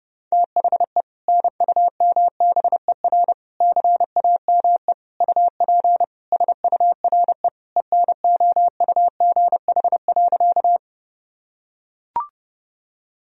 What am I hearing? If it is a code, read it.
Morse code, heard as a keyed tone, about 30 wpm, 700 Hz. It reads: THE NUMBER CAME UP SURE ENOUGH.
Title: Sentences from Top 1000 Words - Review Part 1 - 30wpm